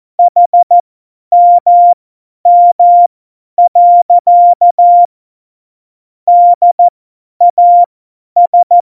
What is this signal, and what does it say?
Morse code, heard as a keyed tone, about 14 wpm, spaced as for 10 wpm, 700 Hz. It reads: HMM. DAS
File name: Morsekurs / Witze / Witz 11. Februar 2020, 50 F70